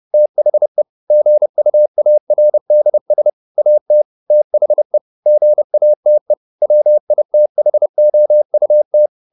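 Morse code, heard as a keyed tone, about 30 wpm, 600 Hz. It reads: THE GUARDS AT THE GATE WITHOUT
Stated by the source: Morse code at 29 wpm, sending Ken Follett's novel Never in English, from CW med Gnister.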